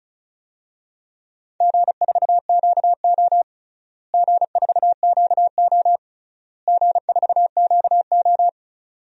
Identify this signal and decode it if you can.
Morse code, heard as a keyed tone, about 34 wpm, 700 Hz. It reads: G4QO G4QO G4QO
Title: DXCC-100 Call Signs - 4 Characters - Repeated 3x - 35wpm